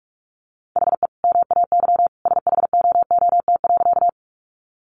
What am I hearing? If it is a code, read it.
Morse code, heard as a keyed tone, about 44 wpm, 700 Hz. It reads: HE MAY SHOOT.
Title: Sentences from Top 1200 Words 45wpm